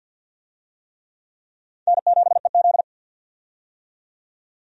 Morse code, heard as a keyed tone, about 51 wpm, 700 Hz. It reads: N7EZ
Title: USA Call Signs - 4 Characters - 50wpm